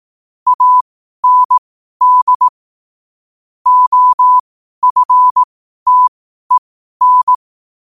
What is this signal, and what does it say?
Morse code, heard as a keyed tone, about 18 wpm, spaced as for 12 wpm, 1000 Hz. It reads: AND OFTEN
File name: qotdfiles / 12wpm